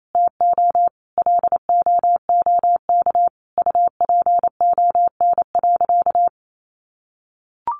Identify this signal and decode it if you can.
Morse code, heard as a keyed tone, about 28 wpm, 700 Hz. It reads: TO LOOK UPON.
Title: Sentences from Top 1400 Words 28wpm